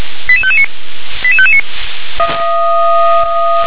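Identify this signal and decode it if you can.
Five-tone selective call (ZVEI), heard as quick runs of five tones, tones of about 70 ms. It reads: 88559 88559
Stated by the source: ZVEI five-tone selective call 88645 received on an SDR.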